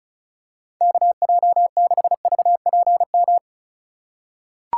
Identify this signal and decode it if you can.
Morse code, heard as a keyed tone, about 36 wpm, 700 Hz. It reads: KJ6VPM
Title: USA Call Signs - 2x Character Spacing - 35wpm